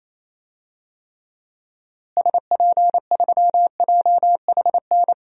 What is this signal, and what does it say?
Morse code, heard as a keyed tone, about 27 wpm, 700 Hz. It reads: SP3JHN
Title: DXCC-100 Call Signs - 6 Characters - Repeated 2x - 28wpm